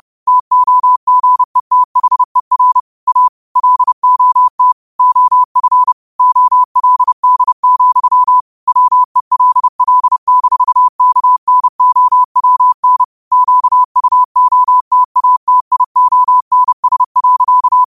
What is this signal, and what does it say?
Morse code, heard as a keyed tone, about 30 wpm, 1000 Hz. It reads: TOGETHER A LOT OF OLD, WELL-KNOWN QUOTATIONS.